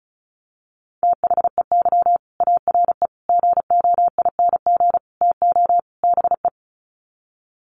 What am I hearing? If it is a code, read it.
Morse code, heard as a keyed tone, about 34 wpm, 700 Hz. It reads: THEY ARE GOING TO BE
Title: Sets of 5 Words Encore2 35wpm